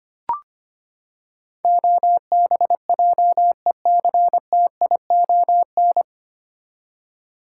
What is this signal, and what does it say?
Morse code, heard as a keyed tone, about 25 wpm, 700 Hz. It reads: OBJECTION
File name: Nine Letter Words - 25wpm